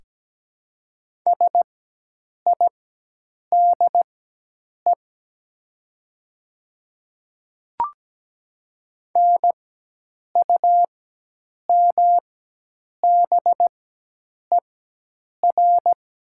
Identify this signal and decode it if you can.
Morse code, heard as a keyed tone, about 17 wpm, spaced as for 8 wpm, 700 Hz. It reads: SIDE NUMBER
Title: 4x Character Spacing - Top 300 Words - 17wpm